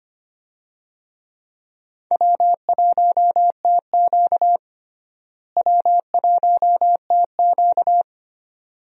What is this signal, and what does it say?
Morse code, heard as a keyed tone, about 25 wpm, 700 Hz. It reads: W1TQ W1TQ
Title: USA Call Signs - 4 Characters - Repeated 2x - 25wpm